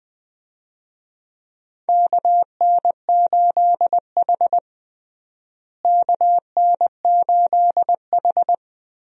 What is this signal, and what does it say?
Morse code, heard as a keyed tone, about 21 wpm, 700 Hz. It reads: KN8H KN8H
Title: USA Call Signs - 4 Characters - Repeated 2x - 20wpm